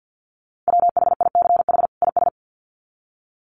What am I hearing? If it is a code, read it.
Morse code, heard as a keyed tone, about 50 wpm, 700 Hz. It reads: WHICH IS
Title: Sets of 2 Words 50wpm